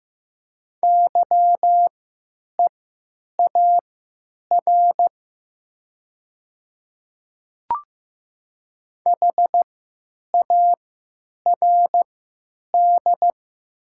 Morse code, heard as a keyed tone, about 15 wpm, spaced as for 9 wpm, 700 Hz. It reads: YEAR HARD
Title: 3x Character Spacing - Top 300 Words - 15wpm